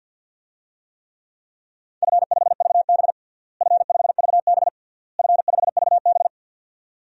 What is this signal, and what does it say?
Morse code, heard as a keyed tone, about 52 wpm, 700 Hz. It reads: F5VB F5VB F5VB
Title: DXCC-100 Call Signs - 4 Characters - Repeated 3x - 50wpm